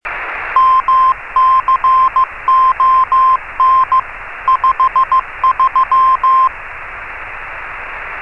Morse code, heard as a keyed tone, about 15 wpm, 1050 Hz, with radio noise. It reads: MCON 53
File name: RS30-CW-Beacon